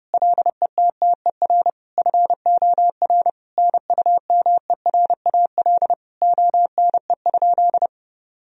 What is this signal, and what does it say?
Morse code, heard as a keyed tone, about 30 wpm, 700 Hz. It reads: LETTER FOR NUMERAL ONE?
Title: Sentences from Top 1000 Words - Review Part 2 - 30wpm